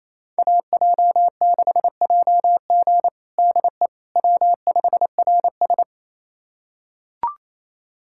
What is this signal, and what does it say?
Morse code, heard as a keyed tone, about 28 wpm, 700 Hz. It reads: AJ6JG DE W5RS
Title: USA Callsign DE Callsign 28wpm